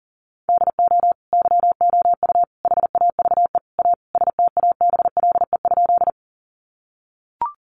Morse code, heard as a keyed tone, about 40 wpm, 700 Hz. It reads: DO YOU HAVE A STABLE?
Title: Sentences from Top 1500 Words 40wpm